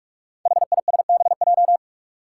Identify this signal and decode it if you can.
Morse code, heard as a keyed tone, about 46 wpm, 700 Hz. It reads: HISBJ
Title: Five Challenging Letter-Number Combinations 45wpm